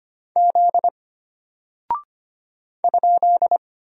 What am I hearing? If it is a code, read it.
Morse code, heard as a keyed tone, about 25 wpm, 700 Hz. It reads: Z ?
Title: Single Character (Letter:Number:Punctuation:Prosign) Alt 25wpm